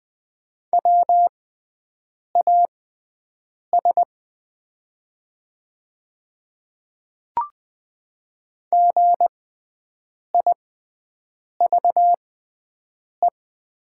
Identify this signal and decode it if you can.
Morse code, heard as a keyed tone, about 20 wpm, spaced as for 7 wpm, 700 Hz. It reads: WAS GIVE
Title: 6x Character Spacing - Top 500 Words - 20wpm